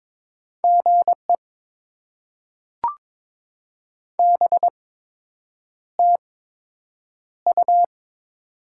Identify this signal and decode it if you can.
Morse code, heard as a keyed tone, about 22 wpm, spaced as for 8 wpm, 700 Hz. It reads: GE BTU
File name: Top 90 QSO Elements - 8x Character Spacing - 22wpm